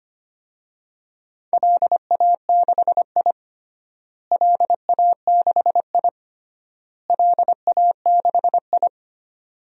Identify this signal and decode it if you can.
Morse code, heard as a keyed tone, about 25 wpm, 700 Hz. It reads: LA6I LA6I LA6I
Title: DXCC-100 Call Signs - 4 Characters - Repeated 3x - 25wpm